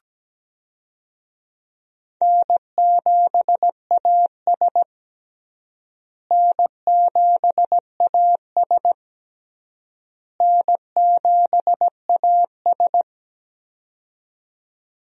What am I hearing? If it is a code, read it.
Morse code, heard as a keyed tone, about 17 wpm, 700 Hz. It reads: N7AS N7AS N7AS
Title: USA Call Signs - 4 Characters - Repeated 3x - 17wpm